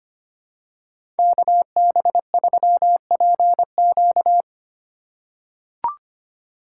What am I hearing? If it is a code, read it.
Morse code, heard as a keyed tone, about 25 wpm, 700 Hz. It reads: KB3PQ
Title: USA Call Signs - 5 Characters - 3x Character Spacing - 25wpm